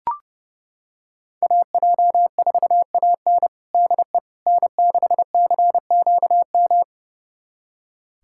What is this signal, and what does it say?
Morse code, heard as a keyed tone, about 32 wpm, 700 Hz. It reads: AJ4AN DE N6CQM
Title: USA Callsign DE Callsign - 5 Characters - 30wpm